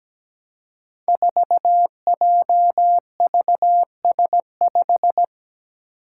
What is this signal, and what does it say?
Morse code, heard as a keyed tone, about 17 wpm, 700 Hz. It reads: 4JVS5
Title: Five Challenging Letter-Number Combinations 17wpm